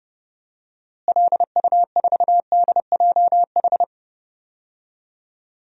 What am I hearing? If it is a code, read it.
Morse code, heard as a keyed tone, about 30 wpm, 700 Hz. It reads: LU4DJH
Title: DXCC-100 Call Signs - 6 Characters - Repeated 2x - 30wpm